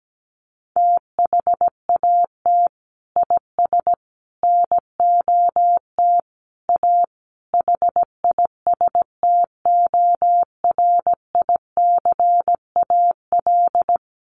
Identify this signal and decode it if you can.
Morse code, heard as a keyed tone, about 17 wpm, 700 Hz. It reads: THAT IS NOT A HISTORICAL
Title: Sentences from Top 600 Words 17wpm